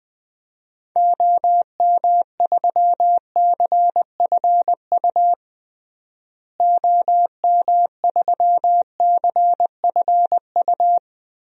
Morse code, heard as a keyed tone, about 20 wpm, 700 Hz. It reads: OM3CFU OM3CFU
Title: DXCC-100 Call Signs - 6 Characters - Repeated 3x - 20wpm